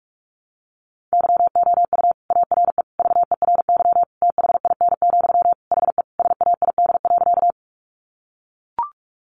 Morse code, heard as a keyed tone, about 44 wpm, 700 Hz. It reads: YOU ARE VERY THIN, HE SAID.
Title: Sentences from Top 1200 Words 45wpm